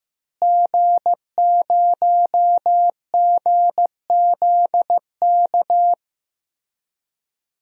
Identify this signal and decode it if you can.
Morse code, heard as a keyed tone, about 15 wpm, 700 Hz. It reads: G0GZK
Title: DXCC-100 Call Signs - 5 Characters - Repeated 3x - 15wpm